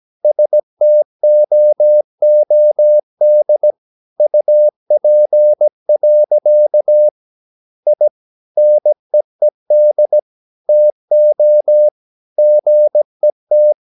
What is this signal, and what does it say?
Morse code, heard as a keyed tone, about 17 wpm, 600 Hz. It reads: STOOD UP. I NEED TO GET
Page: Romanen Never af Ken Follet på engelsk.